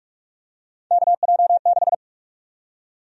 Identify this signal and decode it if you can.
Morse code, heard as a keyed tone, about 44 wpm, 700 Hz. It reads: KJ6
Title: USA Call Sign Prefixes - 45wpm